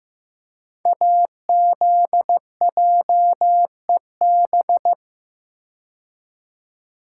Morse code, heard as a keyed tone, about 15 wpm, 700 Hz. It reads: AZJEB
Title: Code Groups - 5 Letter:Numbers - 15wpm